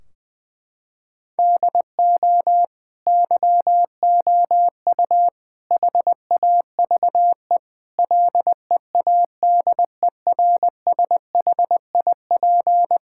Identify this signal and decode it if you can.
Morse code, heard as a keyed tone, about 21 wpm, 700 Hz. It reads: DO YOU HAVE LEADERSHIP
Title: Sentences from Top 1600 Words 20wpm